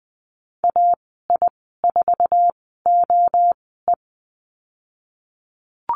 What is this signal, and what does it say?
Morse code, heard as a keyed tone, about 20 wpm, spaced as for 15 wpm, 700 Hz. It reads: AI4OE
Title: 048 Lesson 9g - Number 4 - Call Signs 20wpm